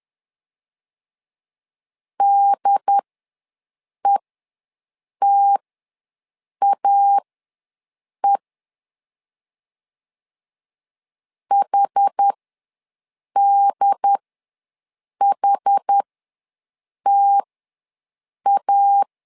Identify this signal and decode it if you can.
Morse code, heard as a keyed tone, about 10 wpm, spaced as for 6 wpm, 800 Hz. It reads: DETAE HDHTA
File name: LEZIONI / 2 UHDN / 2 UHD ws 10wpm 2058